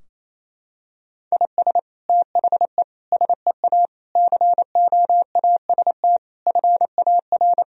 Morse code, heard as a keyed tone, about 27 wpm, 700 Hz. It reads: IS THE SEA COAST FAR
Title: Sentences from Top 1000 Words - Review Part 4 - 28wpm